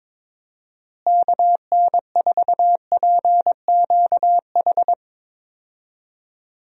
Morse code, heard as a keyed tone, about 22 wpm, 700 Hz. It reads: KN4PQH
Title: USA Call Signs - 6 Characters - Repeated 2x - 22wpm